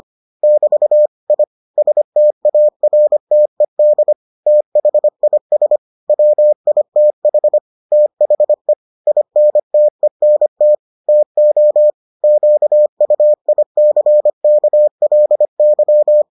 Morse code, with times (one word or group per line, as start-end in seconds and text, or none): 0.4-1.1 <BT>
1.3-1.4 I
1.8-4.1 STARTED
4.5-5.8 THIS
6.1-7.6 WITH
7.9-8.7 THE
9.1-10.8 INTENT
11.1-11.9 TO
12.2-16.2 QUICKLY